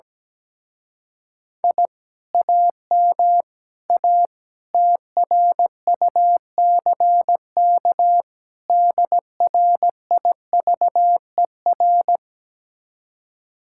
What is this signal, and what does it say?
Morse code, heard as a keyed tone, about 17 wpm, 700 Hz. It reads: I AM A TRUCK DRIVER
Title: Speed-Racing Common QSO Phrases Encore - 17wpm